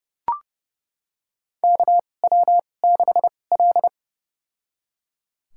Morse code, heard as a keyed tone, about 30 wpm, spaced as for 22 wpm, 700 Hz. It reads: KW6L
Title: USA Call Signs - 2x Character Spacing - 30wpm